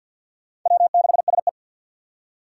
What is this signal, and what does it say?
Morse code, heard as a keyed tone, about 49 wpm, 700 Hz. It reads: W6SE
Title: USA Call Signs - 2x Character Spacing - 50wpm